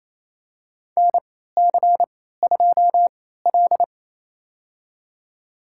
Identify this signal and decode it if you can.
Morse code, heard as a keyed tone, about 28 wpm, spaced as for 16 wpm, 700 Hz. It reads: NC2L
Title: USA Call Signs - 4 Characters - 3x Character Spacing - 28wpm